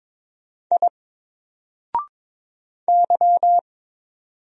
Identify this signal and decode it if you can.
Morse code, heard as a keyed tone, about 22 wpm, 700 Hz. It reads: I Y
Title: Single Character (Letter:Number:Punctuation:Prosign) 22wpm